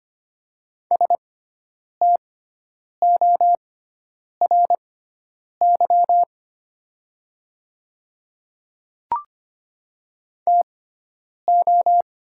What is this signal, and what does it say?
Morse code, heard as a keyed tone, about 25 wpm, spaced as for 9 wpm, 700 Hz. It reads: STORY TO
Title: 6x Character Spacing - Top 500 Words - 25wpm